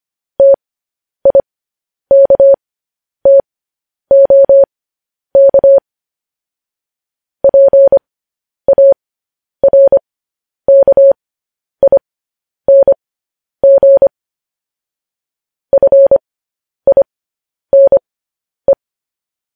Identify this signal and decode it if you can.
Morse code, heard as a keyed tone, about 25 wpm, spaced as for 10 wpm, 550 Hz. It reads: TIKTOK PARKING FINE
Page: News Headlines in Morse Code at 10 WPM